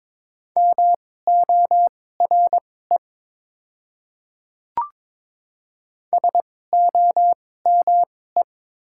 Morse code, heard as a keyed tone, about 22 wpm, spaced as for 16 wpm, 700 Hz. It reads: MORE SOME
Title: 2x Character Spacing - Top 100 Words - 22wpm